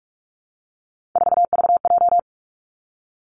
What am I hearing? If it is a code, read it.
Morse code, heard as a keyed tone, about 46 wpm, 700 Hz. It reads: 4VJ
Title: Three Challenging Letter-Number Combinations 45wpm